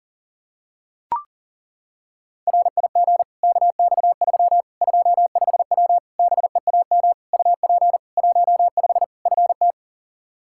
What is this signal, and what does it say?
Morse code, heard as a keyed tone, about 37 wpm, 700 Hz. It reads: RIG KX3 25W BEAM UP 15 FT
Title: K1USN SST QSO Practice 40wpm